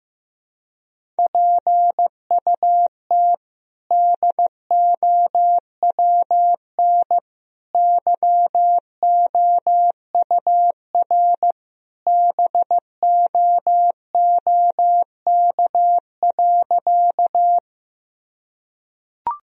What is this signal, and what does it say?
Morse code, heard as a keyed tone, about 15 wpm, 700 Hz. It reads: PUT DOWN YOUR BOOK.
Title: Sentences from Top 1000 Words - Review Part 1 - 15wpm